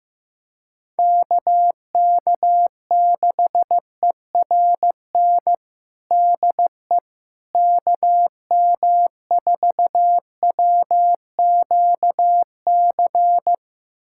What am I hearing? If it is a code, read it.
Morse code, heard as a keyed tone, about 15 wpm, 700 Hz. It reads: KK6ERN DE KM4WQC
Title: USA Callsign DE Callsign - 6 Characters - 15wpm